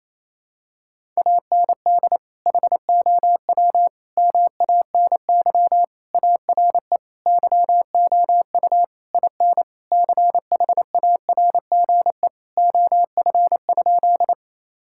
Morse code, heard as a keyed tone, about 27 wpm, 700 Hz. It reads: AND HOW MANY ARE YOU IN CHARGE OF?
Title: Sentences from Top 1000 Words - Review Part 6 - 28wpm